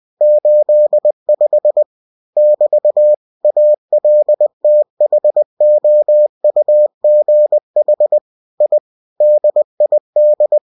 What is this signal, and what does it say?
Morse code, heard as a keyed tone, about 20 wpm, 600 Hz. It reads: 85 <BT> ALTHOUGH I DID
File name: cw-books / winnie-the-pooh / winnie-20wpm-page-0085